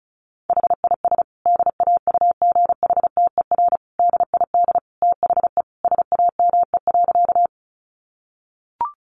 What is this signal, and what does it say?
Morse code, heard as a keyed tone, about 34 wpm, 700 Hz. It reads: HIS DAUGHTER DID THE SAME.
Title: Sentences from Top 1500 Words 35wpm